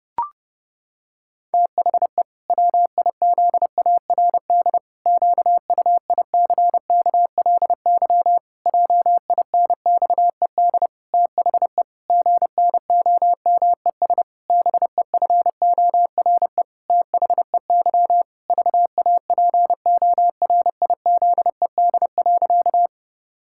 Morse code, heard as a keyed tone, about 32 wpm, 700 Hz. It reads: THE WIZARD QUICKLY JINXED THE GNOMES BEFORE THEY VAPORIZED.
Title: Pangram Challenge v2 - 30wpm